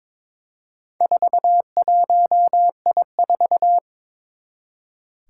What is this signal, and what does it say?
Morse code, heard as a keyed tone, about 21 wpm, 700 Hz. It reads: 41I4
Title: Four Challenging Letter-Number Combinations 22wpm